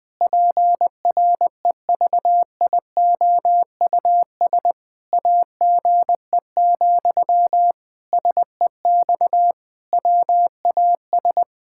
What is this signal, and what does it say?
Morse code, heard as a keyed tone, about 20 wpm, 700 Hz. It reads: PREVIOUS AGE, SEX WAS
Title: Sentences from Top 1600 Words 20wpm